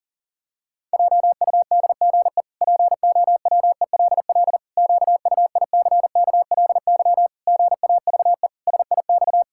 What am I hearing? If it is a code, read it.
Morse code, heard as a keyed tone, about 40 wpm, 700 Hz. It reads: JUDGE POWELL QUICKLY GAVE SIX
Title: Pangram Challenge v2 - 40wpm